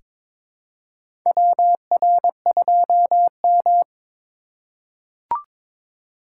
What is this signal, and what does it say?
Morse code, heard as a keyed tone, about 22 wpm, 700 Hz. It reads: WR2M
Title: USA Call Signs - 4x Character Spacing - 22wpm